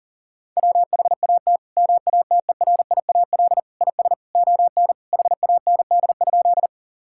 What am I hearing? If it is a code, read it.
Morse code, heard as a keyed tone, about 40 wpm, 700 Hz. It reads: WHAT MATERIAL IS ON HAND?
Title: Sentences from Top 1000 Words - Review Part 5 - 40wpm